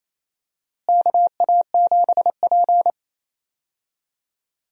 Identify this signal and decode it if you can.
Morse code, heard as a keyed tone, about 28 wpm, 700 Hz. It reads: KA7P
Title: USA Call Signs - 4 Characters - Repeated 3x - 28wpm